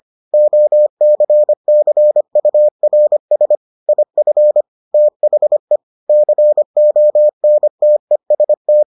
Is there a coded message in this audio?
Yes: Morse code from 0.3 to 8.8 s, heard as a keyed tone, about 25 wpm, 600 Hz. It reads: OCCURS IF THE CONTEST